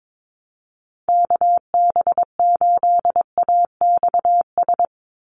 Morse code, heard as a keyed tone, about 22 wpm, 700 Hz. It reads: KB8AXS
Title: USA Call Signs 22wpm